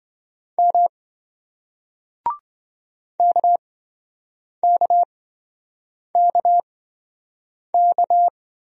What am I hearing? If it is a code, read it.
Morse code, heard as a keyed tone, about 28 wpm, spaced as for 7 wpm, 700 Hz. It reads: M KKKK